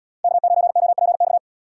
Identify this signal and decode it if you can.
Morse code, heard as a keyed tone, about 77 wpm, 700 Hz. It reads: K9OPX